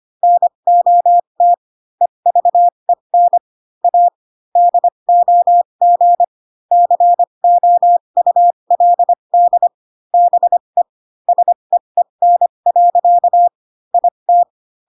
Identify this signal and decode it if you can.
Morse code, heard as a keyed tone, about 25 wpm, spaced as for 22 wpm, 700 Hz. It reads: NOT EVEN A DOG COULD BE SEEN. IT
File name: Pinocchio / Pinocchio ch6 18 25 ik1hge